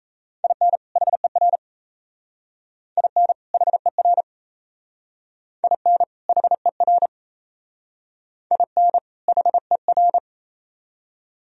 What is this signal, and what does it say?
Morse code, heard as a keyed tone, about 34 wpm, 700 Hz. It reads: IN HER IN HER IN HER IN HER
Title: Speed-Racing - Sets of 2 Words 28wpm